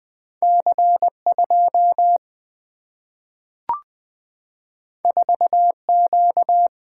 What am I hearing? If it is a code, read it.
Morse code, heard as a keyed tone, about 20 wpm, 700 Hz. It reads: C 2 4 Q
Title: Single-Letter Single-Number 20wpm